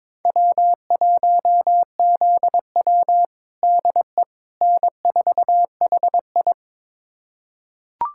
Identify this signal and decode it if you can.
Morse code, heard as a keyed tone, about 21 wpm, 700 Hz. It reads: W1ZW DE N4HI
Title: USA Callsign DE Callsign - 4 Characters - 22wpm